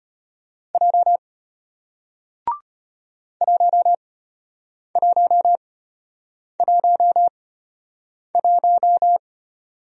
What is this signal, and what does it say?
Morse code, heard as a keyed tone, about 33 wpm, spaced as for 8 wpm, 700 Hz. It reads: J 1111